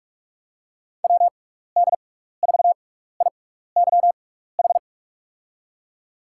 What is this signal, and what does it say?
Morse code, heard as a keyed tone, about 46 wpm, spaced as for 16 wpm, 700 Hz. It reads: WD4IYH
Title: USA Call Signs - 6 Characters - 6x Character Spacing - 45wpm